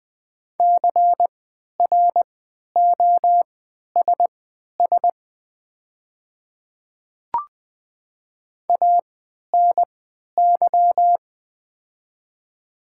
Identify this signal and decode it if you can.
Morse code, heard as a keyed tone, about 20 wpm, spaced as for 11 wpm, 700 Hz. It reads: CROSS ANY
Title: 3x Character Spacing - Top 300 Words - 20wpm